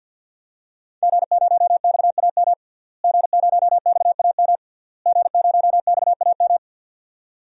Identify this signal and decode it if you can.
Morse code, heard as a keyed tone, about 46 wpm, 700 Hz. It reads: G0XAM G0XAM G0XAM
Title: DXCC-100 Call Signs - 5 Characters - Repeated 3x - 50wpm